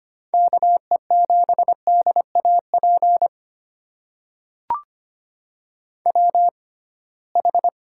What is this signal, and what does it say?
Morse code, heard as a keyed tone, about 25 wpm, spaced as for 19 wpm, 700 Hz. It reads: KE7DAP WH
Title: USA Call Signs - 6 Characters - 6x Character Spacing - 25wpm